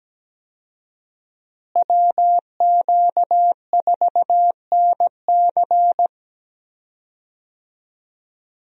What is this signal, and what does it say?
Morse code, heard as a keyed tone, about 17 wpm, 700 Hz. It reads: WQ4NC
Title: USA Call Signs - 5 Characters - Repeated 3x - 17wpm